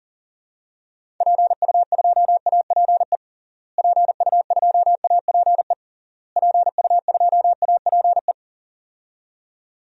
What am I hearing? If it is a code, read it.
Morse code, heard as a keyed tone, about 40 wpm, 700 Hz. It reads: PU2APE PU2APE PU2APE
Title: DXCC-100 Call Signs - 6 Characters - Repeated 3x - 40wpm